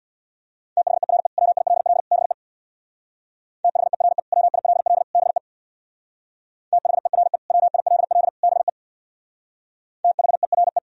Morse code, heard as a keyed tone, about 64 wpm, 700 Hz. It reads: THERE WILL BE THERE WILL BE THERE WILL BE THERE